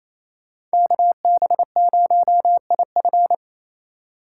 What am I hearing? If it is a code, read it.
Morse code, heard as a keyed tone, about 33 wpm, 700 Hz. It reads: KB0IF
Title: Speed Racing - USA 2x2 Call Signs 28wpm